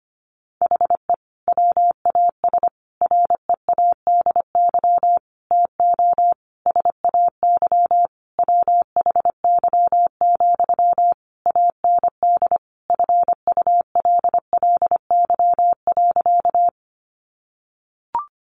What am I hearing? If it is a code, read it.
Morse code, heard as a keyed tone, about 25 wpm, 700 Hz. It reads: HE WAS READY TO SAY WHY, AND FULLY.
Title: Sentences from Top 1300 Words 25wpm